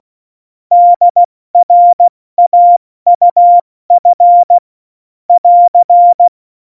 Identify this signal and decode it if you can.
Morse code, heard as a keyed tone, about 16 wpm, spaced as for 14 wpm, 700 Hz. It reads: DRAUF <AR>
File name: Morsekurs / Witze / Witz 10. Oktober 17, Tempo 70